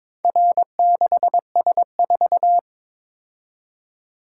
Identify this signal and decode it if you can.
Morse code, heard as a keyed tone, about 22 wpm, 700 Hz. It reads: R6S4
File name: practice / 22wpm / Four Letter-Number Combinations - 22wpm